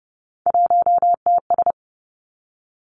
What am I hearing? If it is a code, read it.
Morse code, heard as a keyed tone, about 30 wpm, 700 Hz. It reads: 1TS
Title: Three Challenging Letter-Number Combinations 30wpm